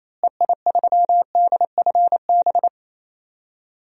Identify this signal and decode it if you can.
Morse code, heard as a keyed tone, about 28 wpm, 700 Hz. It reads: EI3DFB
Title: DXCC-100 Call Signs - 6 Characters - Repeated 3x - 28wpm